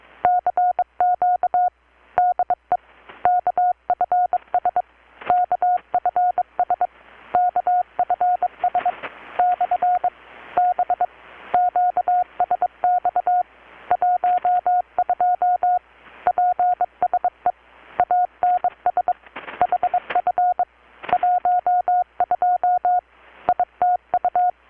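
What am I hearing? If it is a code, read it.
Morse code, heard as a keyed tone, about 22 wpm, 720 Hz. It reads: CQ DE KFS KFS KFS / B QSX 12 PSE ANS HF 12 ITU
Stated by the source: Morse from KPH's Night of Nights XX maritime radiotelegraphy event, received in Chiapas, Mexico about 3,800 km away on a Kenwood R-600 with a grounded 10 m long-wire antenna.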